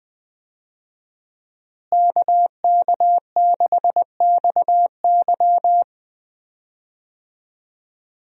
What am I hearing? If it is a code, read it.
Morse code, heard as a keyed tone, about 20 wpm, 700 Hz. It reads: KK6XY
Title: USA Call Signs - 5 Characters - Repeated 2x - 20wpm